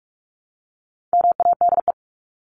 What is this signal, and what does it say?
Morse code, heard as a keyed tone, about 44 wpm, 700 Hz. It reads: MADE
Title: Top 300 Words Repeated 45wpm